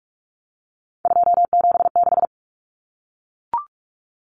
Morse code, heard as a keyed tone, about 44 wpm, 700 Hz. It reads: 276
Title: Common Cut Numbers - 3-Digit 45wpm